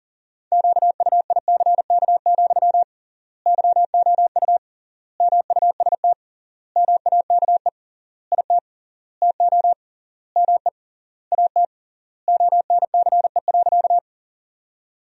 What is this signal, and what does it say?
Morse code, heard as a keyed tone, about 37 wpm, 700 Hz. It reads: QUICK, YOU MUST MAKE IT TO ME AT ONCE.
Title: 3x Word-Spacing - Sentences from Top 500 Words - 40wpm